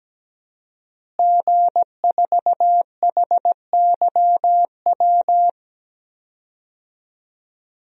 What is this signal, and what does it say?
Morse code, heard as a keyed tone, about 17 wpm, 700 Hz. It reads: G4HYW
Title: DXCC-100 Call Signs - 5 Characters - Repeated 3x - 17wpm